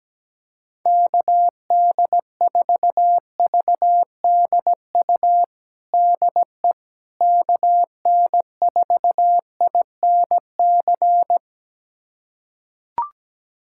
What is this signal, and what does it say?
Morse code, heard as a keyed tone, about 17 wpm, 700 Hz. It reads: KD4VDU DE KN4INC
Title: USA Callsign DE Callsign - 6 Characters - 17wpm